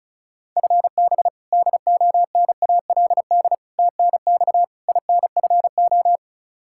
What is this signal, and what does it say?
Morse code, heard as a keyed tone, about 36 wpm, 700 Hz. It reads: FB DONALD TNX INFO
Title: Practice QSO - Exchange 3 - More Info - 35wpm